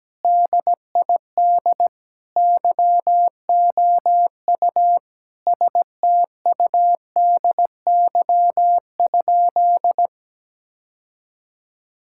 Morse code, heard as a keyed tone, about 17 wpm, 700 Hz. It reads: DID YOU STUDY?
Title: Sentences from Top 1000 Words - Review Part 6 - 17wpm